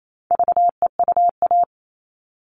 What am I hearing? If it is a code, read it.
Morse code, heard as a keyed tone, about 28 wpm, 700 Hz. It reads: VEUA
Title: All Cut Numbers - 4-Digit 28wpm